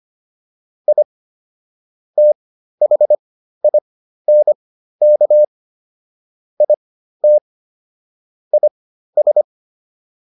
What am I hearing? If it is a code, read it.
Morse code, heard as a keyed tone, about 25 wpm, spaced as for 13 wpm, 600 Hz. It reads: I THINK IT IS